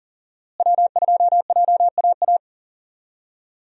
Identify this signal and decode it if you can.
Morse code, heard as a keyed tone, about 37 wpm, 700 Hz. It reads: W2JAA
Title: USA Call Signs - 5 Characters - Repeated 2x - 40wpm